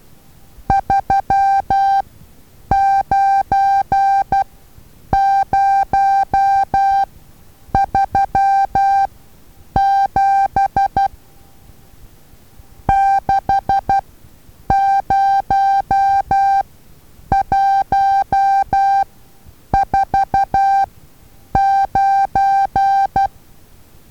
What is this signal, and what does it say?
Morse code, heard as a keyed tone, about 12 wpm, spaced as for 8 wpm, 790 Hz. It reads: 39037 60149
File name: Taviro HA6VA A